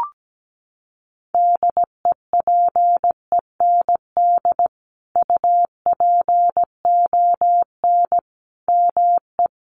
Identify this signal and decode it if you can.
Morse code, heard as a keyed tone, about 17 wpm, 700 Hz. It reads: DEPEND UPON ME
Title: Sentences from Top 1300 Words 17wpm